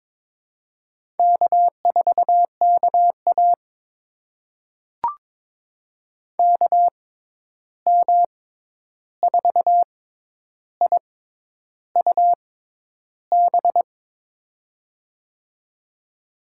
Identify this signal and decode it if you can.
Morse code, heard as a keyed tone, about 22 wpm, spaced as for 10 wpm, 700 Hz. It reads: K4KA KM4IUB
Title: USA Call Signs - 6x Character Spacing - 22wpm